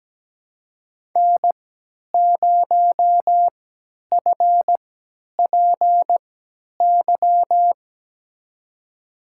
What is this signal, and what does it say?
Morse code, heard as a keyed tone, about 17 wpm, spaced as for 10 wpm, 700 Hz. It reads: N0FPY
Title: USA Call Signs - 5 Characters - 3x Character Spacing - 17wpm